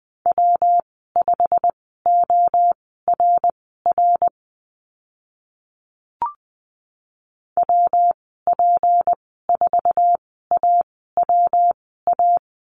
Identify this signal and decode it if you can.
Morse code, heard as a keyed tone, about 20 wpm, spaced as for 14 wpm, 700 Hz. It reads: W5ORR WP4AWA
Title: 161 Lesson 26g - Letter g - Call Signs 20wpm